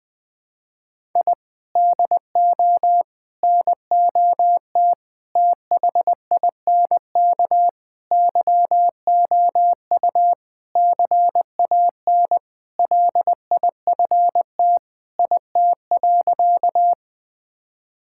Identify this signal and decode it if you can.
Morse code, heard as a keyed tone, about 20 wpm, 700 Hz. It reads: I DO NOT THINK YOU CAN LIFT IT.
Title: Sentences with 5-Word Phrases - 20wpm